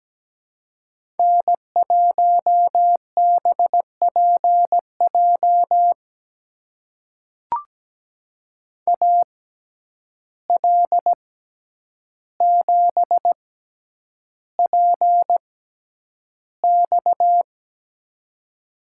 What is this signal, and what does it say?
Morse code, heard as a keyed tone, about 17 wpm, spaced as for 9 wpm, 700 Hz. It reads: N1BPJ AL7PX
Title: USA Call Signs - 5 Characters - 6x Character Spacing - 17wpm